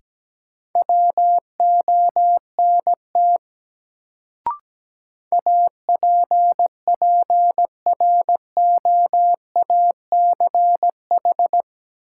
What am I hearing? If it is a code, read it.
Morse code, heard as a keyed tone, about 17 wpm, 700 Hz. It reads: WONT APPROACH
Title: Sentences from Top 1000 Words - Review Part 6 - New Word Drill - 17wpm